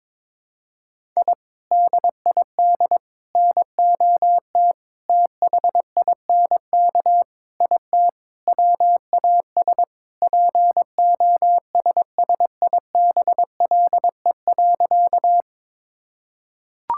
Morse code, heard as a keyed tone, about 22 wpm, 700 Hz. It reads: I DID NOT THINK IT WAS POSSIBLE.
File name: practice / 22wpm / Sentences with 4-Word Phrases - 22wpm